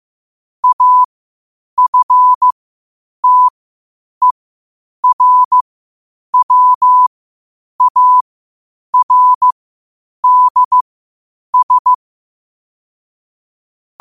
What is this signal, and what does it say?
Morse code, heard as a keyed tone, about 15 wpm, spaced as for 8 wpm, 1000 Hz. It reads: AFTERWARDS
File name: qotdfiles / 8wpm